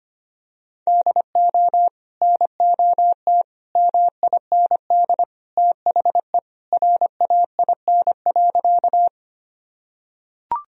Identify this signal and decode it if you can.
Morse code, heard as a keyed tone, about 25 wpm, 700 Hz. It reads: DO NOT MIND THE RAIN.
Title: Sentences from Top 1000 Words - Review Part 1 - 25wpm